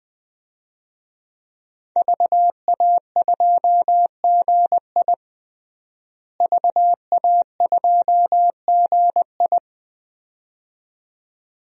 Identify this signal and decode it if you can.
Morse code, heard as a keyed tone, about 21 wpm, 700 Hz. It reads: VA2GI VA2GI
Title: DXCC-100 Call Signs - 5 Characters - Repeated 2x - 20wpm